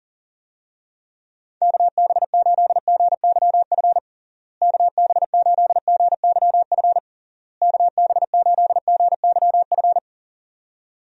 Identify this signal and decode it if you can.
Morse code, heard as a keyed tone, about 40 wpm, 700 Hz. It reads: KB8GYF KB8GYF KB8GYF
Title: USA Call Signs - 6 Characters - Repeated 3x - 40wpm